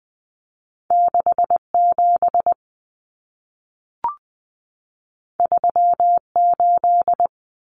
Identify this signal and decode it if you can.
Morse code, heard as a keyed tone, about 20 wpm, 700 Hz. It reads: 67 38
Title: Common Cut Numbers - 2-Digit 20wpm